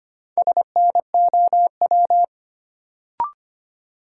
Morse code, heard as a keyed tone, about 25 wpm, 700 Hz. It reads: SNOW
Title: Sentences from Top 500 Words - New Word Drill - 25wpm